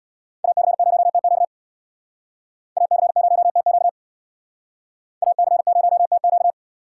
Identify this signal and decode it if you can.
Morse code, heard as a keyed tone, about 61 wpm, 700 Hz. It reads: AC0TQ AC0TQ AC0TQ
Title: Speed Racing - USA 2x2 Call Signs 50wpm